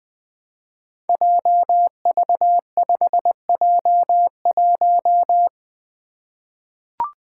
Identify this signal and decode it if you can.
Morse code, heard as a keyed tone, about 20 wpm, 700 Hz. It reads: JV5J1
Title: Five Challenging Letter-Number Combinations 20wpm